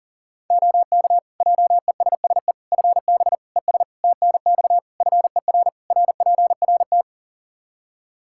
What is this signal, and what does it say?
Morse code, heard as a keyed tone, about 40 wpm, 700 Hz. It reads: OK JESSE FB ES TNX FER RPRT
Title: Practice QSO - Exchange 2 - Info - 40wpm